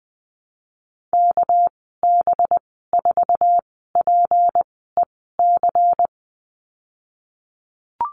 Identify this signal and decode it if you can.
Morse code, heard as a keyed tone, about 19 wpm, spaced as for 14 wpm, 700 Hz. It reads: KB4PEC
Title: 213 Lesson 34g - Number 8 - Call Signs 20wpm